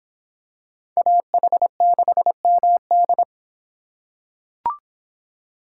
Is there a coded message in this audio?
Yes: Morse code from 1.0 to 3.2 s, heard as a keyed tone, about 26 wpm, 700 Hz. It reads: AH6MD